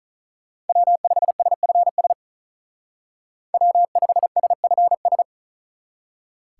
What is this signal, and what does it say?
Morse code, heard as a keyed tone, about 46 wpm, 700 Hz. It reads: W5SFS W5SFS